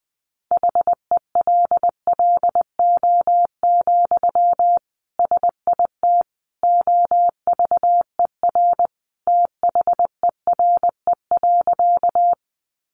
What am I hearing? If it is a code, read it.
Morse code, heard as a keyed tone, about 20 wpm, 700 Hz. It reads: HELLO, SIT OVER THERE.
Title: Sentences from Top 1100 Words 20wpm